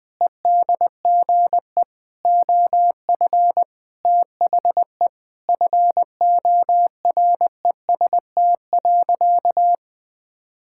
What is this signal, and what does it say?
Morse code, heard as a keyed tone, about 21 wpm, 700 Hz. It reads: EDGE OF THE FOREST.
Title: Sentences with 5-Word Phrases - 20wpm